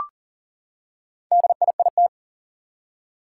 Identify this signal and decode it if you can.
Morse code, heard as a keyed tone, about 42 wpm, 700 Hz. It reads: DIIT
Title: Four Challenging Letter-Number Combinations 40wpm